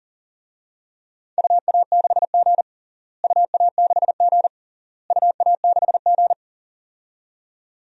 Morse code, heard as a keyed tone, about 37 wpm, 700 Hz. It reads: UA6G UA6G UA6G
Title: DXCC-100 Call Signs - 4 Characters - Repeated 3x - 40wpm